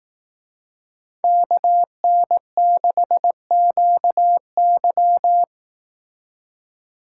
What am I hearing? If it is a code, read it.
Morse code, heard as a keyed tone, about 18 wpm, 700 Hz. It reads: KN6QY
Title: Speed Racing - USA 2x2 Call Signs 15wpm